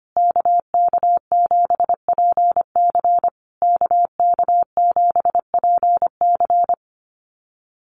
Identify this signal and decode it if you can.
Morse code, heard as a keyed tone, about 25 wpm, 700 Hz. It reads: KK7PC KK7PC
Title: USA Call Signs Repeated 3x 25wpm